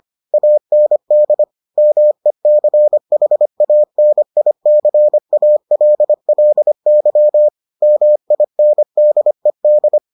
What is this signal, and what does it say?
Morse code, heard as a keyed tone, about 25 wpm, 600 Hz. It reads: AND MECHANICALLY MINDED